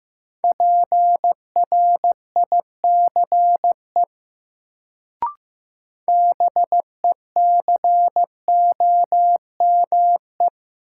Morse code, heard as a keyed tone, about 15 wpm, 700 Hz. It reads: PRICE BECOME
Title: Sentences from Top 600 Words - New Word Drill - 15wpm